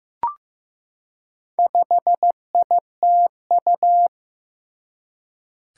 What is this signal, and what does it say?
Morse code, heard as a keyed tone, about 15 wpm, 700 Hz. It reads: 5ITU
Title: Four Challenging Letter-Number Combinations 15wpm